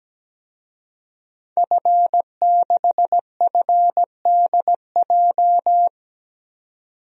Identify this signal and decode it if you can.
Morse code, heard as a keyed tone, about 17 wpm, 700 Hz. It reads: F6FDJ
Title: DXCC-100 Call Signs - 5 Characters - 17wpm